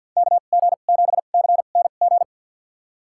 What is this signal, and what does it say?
Morse code, heard as a keyed tone, about 47 wpm, spaced as for 35 wpm, 700 Hz. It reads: KG7CNG
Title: USA Call Signs - 2x Character Spacing - 50wpm